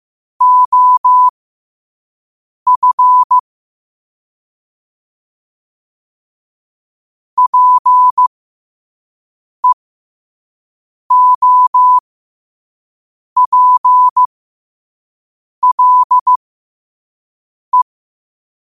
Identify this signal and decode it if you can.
Morse code, heard as a keyed tone, about 15 wpm, spaced as for 5 wpm, 1000 Hz. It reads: OF PEOPLE
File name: qotdfiles / 5wpm